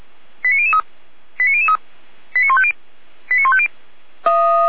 Five-tone selective call (ZVEI), heard as quick runs of five tones, tones of about 70 ms. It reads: 89003 89003 89150 89150
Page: ZVEI